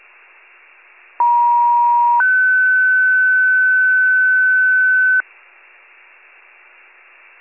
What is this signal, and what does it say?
Two-tone page, heard as a one-second tone, then a three-second tone, about 950 then 1500 Hz. It is heard from 1.2 to 5.2 s.